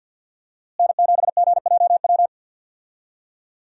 Morse code, heard as a keyed tone, about 47 wpm, 700 Hz. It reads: N7GJW
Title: USA Call Signs - 5 Characters - Repeated 2x - 50wpm